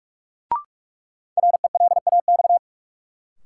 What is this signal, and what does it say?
Morse code, heard as a keyed tone, about 41 wpm, 700 Hz. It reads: RELAX
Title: Sentences from Top 1400 Words - New Word Drill Alt - 45wpm